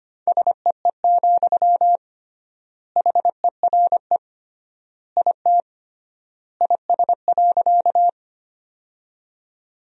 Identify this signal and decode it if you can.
Morse code, heard as a keyed tone, about 25 wpm, 700 Hz. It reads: SEE, HERE IT IS.
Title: 3x Word-Spacing - Sentences from Top 200 Words - 25wpm